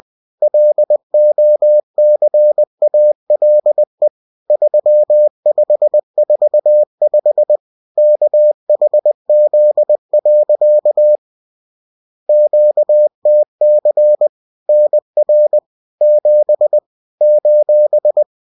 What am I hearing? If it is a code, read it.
Morse code, heard as a keyed tone, about 20 wpm, 600 Hz. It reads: LOCALE 3545 KHZ. QTC NR 7 :